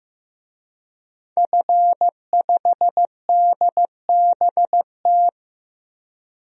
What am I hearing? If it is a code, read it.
Morse code, heard as a keyed tone, about 15 wpm, 700 Hz. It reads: F5DBT
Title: DXCC-100 Call Signs - 5 Characters - 15wpm